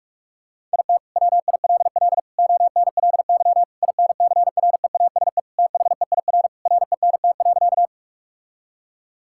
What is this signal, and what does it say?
Morse code, heard as a keyed tone, about 46 wpm, 700 Hz. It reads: IT WILL ONLY INCREASE THEIR RENT.
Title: Sentences from Top 1000 Words - Review Part 5 - 45wpm